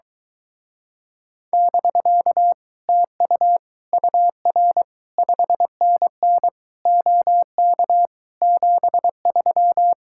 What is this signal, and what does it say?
Morse code, heard as a keyed tone, about 24 wpm, 700 Hz. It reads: <BK> TU UR 5NN OK 73